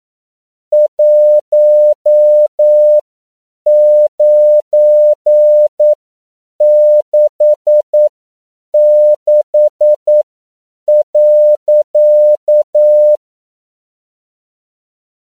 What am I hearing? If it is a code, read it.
Morse code, heard as a keyed tone, about 9 wpm, spaced as for 7 wpm, 600 Hz. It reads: 1966.
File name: CW / 10PPM